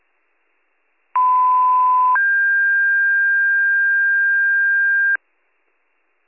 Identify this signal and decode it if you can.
Two-tone page, heard as a one-second tone, then a three-second tone, about 1010 then 1600 Hz. It is heard from 1.2 to 5.2 s.